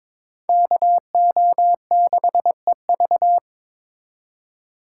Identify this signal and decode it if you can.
Morse code, heard as a keyed tone, about 26 wpm, 700 Hz. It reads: KO6EV
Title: Speed Racing - USA 2x2 Call Signs 22wpm